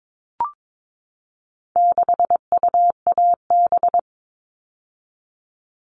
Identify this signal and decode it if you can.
Morse code, heard as a keyed tone, about 22 wpm, 700 Hz. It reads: 6UAB
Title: All Cut Numbers - 4-Digit 22wpm